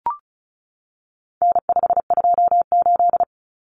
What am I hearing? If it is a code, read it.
Morse code, heard as a keyed tone, about 34 wpm, 700 Hz. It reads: N528
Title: Common Cut Numbers - 4-Digit 35wpm